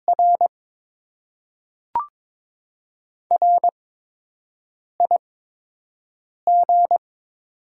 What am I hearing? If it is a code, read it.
Morse code, heard as a keyed tone, about 21 wpm, spaced as for 6 wpm, 700 Hz. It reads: R RIG
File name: Top 20 QSO Elements - 8x Character Spacing - 22wpm